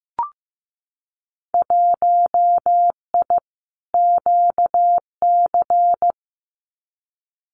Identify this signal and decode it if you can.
Morse code, heard as a keyed tone, about 15 wpm, 700 Hz. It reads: 1I QC
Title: Winter Field Day Exchanges 15wpm